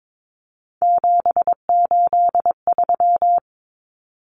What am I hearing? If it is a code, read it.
Morse code, heard as a keyed tone, about 22 wpm, 700 Hz. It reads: 783
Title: Common Cut Numbers - 3-Digit 22wpm